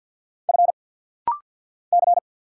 Morse code, heard as a keyed tone, about 50 wpm, 700 Hz. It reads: F C
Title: Single Letter-Number Alt - Rapid-Fire 50wpm